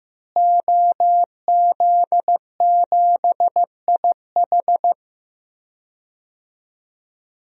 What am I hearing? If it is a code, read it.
Morse code, heard as a keyed tone, about 15 wpm, 700 Hz. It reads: OZ7IH
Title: DXCC-100 Call Signs - 5 Characters - Repeated 2x - 15wpm